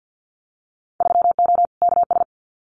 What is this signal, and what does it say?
Morse code, heard as a keyed tone, about 50 wpm, 700 Hz. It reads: 3O KS
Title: Winter Field Day Exchanges 50wpm